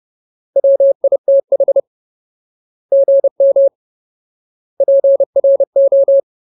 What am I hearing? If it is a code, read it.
Morse code, heard as a keyed tone, about 30 wpm, 550 Hz. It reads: WITH GM PRO
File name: Ultimate Word List / 30WPM Ultimate